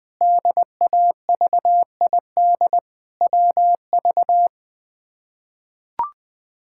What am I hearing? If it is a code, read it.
Morse code, heard as a keyed tone, about 20 wpm, 700 Hz. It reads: DAVID WV
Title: K1USN SST QSO Practice - 20wpm